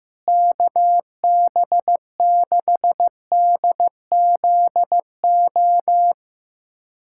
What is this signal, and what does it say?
Morse code, heard as a keyed tone, about 15 wpm, 700 Hz. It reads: KB6DZO
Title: USA Callsign DE Callsign 15wpm